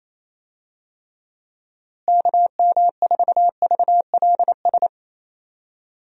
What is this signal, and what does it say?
Morse code, heard as a keyed tone, about 27 wpm, 700 Hz. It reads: KM4VLS
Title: USA Call Signs - 6 Characters - Repeated 3x - 28wpm